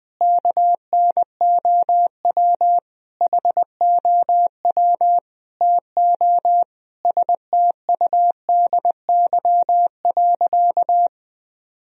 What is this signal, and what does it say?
Morse code, heard as a keyed tone, about 20 wpm, 700 Hz. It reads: KNOW HOW TO STUDY.
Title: Sentences from Top 1000 Words - Review Part 1 - 20wpm